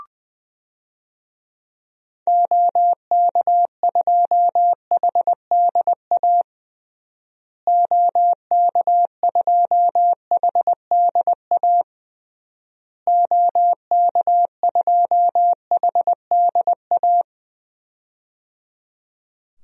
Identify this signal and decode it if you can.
Morse code, heard as a keyed tone, about 21 wpm, 700 Hz. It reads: OK2HDA OK2HDA OK2HDA
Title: DXCC-100 Call Signs - 6 Characters - Repeated 3x - 20wpm